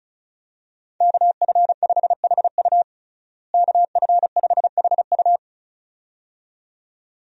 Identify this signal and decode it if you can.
Morse code, heard as a keyed tone, about 35 wpm, 700 Hz. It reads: KF5HU KF5HU
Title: USA Call Signs - 5 Characters - Repeated 2x - 35wpm